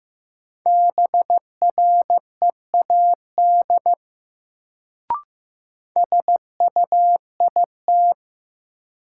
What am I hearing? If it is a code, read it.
Morse code, heard as a keyed tone, about 15 wpm, 700 Hz. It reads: BREAD SUIT
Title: Sentences from Top 1600 Words - New Word Drill Alt - 15wpm